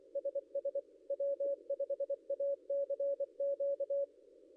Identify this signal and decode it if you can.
Morse code, heard as a keyed tone, about 24 wpm, 560 Hz. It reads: SS W5ACQ
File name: contests / ss-cw-k5tr-op-n5rz -0800